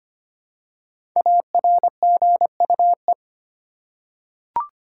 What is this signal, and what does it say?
Morse code, heard as a keyed tone, about 25 wpm, 700 Hz. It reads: ARGUE
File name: 5 Letter Words - First 3 Letters Given - 25wpm